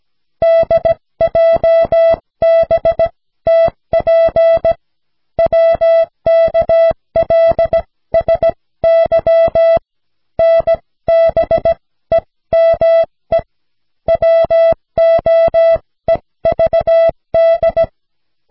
Morse code, heard as a keyed tone, about 16 wpm, spaced as for 14 wpm, 660 Hz. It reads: DJBTP WKLSY NBEME WOEVD
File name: cw / snelheid 4